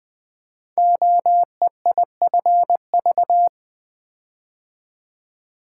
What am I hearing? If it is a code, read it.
Morse code, heard as a keyed tone, about 20 wpm, 700 Hz. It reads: OEIFV
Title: Code Groups - 5 Letters - 20wpm